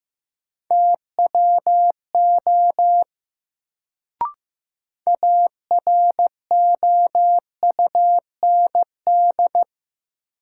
Morse code, heard as a keyed tone, about 15 wpm, 700 Hz. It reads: TWO AROUND
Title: Sets of 2 Words - New Word Drill - 15wpm